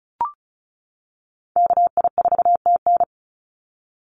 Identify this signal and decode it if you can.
Morse code, heard as a keyed tone, about 34 wpm, 700 Hz. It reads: KI4TN
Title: NA SOTA Activator Callsigns 35wpm